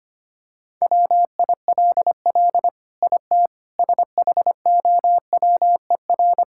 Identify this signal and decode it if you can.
Morse code, heard as a keyed tone, about 25 wpm, 700 Hz. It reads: WILL IT SHOWER
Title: Sentences from Top 1600 Words 25wpm